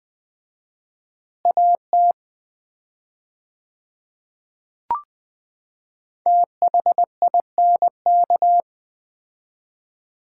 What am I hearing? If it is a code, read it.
Morse code, heard as a keyed tone, about 20 wpm, 690 Hz. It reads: AT THINK
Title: 4x Word-Spacing - Sets of 2 Words - 20wpm